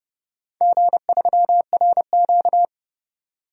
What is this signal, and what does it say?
Morse code, heard as a keyed tone, about 32 wpm, 700 Hz. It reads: G3RQ
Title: DXCC-100 Call Signs - 4 Characters - 30wpm